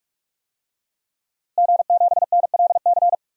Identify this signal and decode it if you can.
Morse code, heard as a keyed tone, about 46 wpm, 700 Hz. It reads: G7NLC
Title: DXCC-100 Call Signs - 5 Characters - Repeated 2x - 45wpm